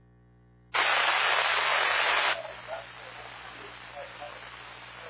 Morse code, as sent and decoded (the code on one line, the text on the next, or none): none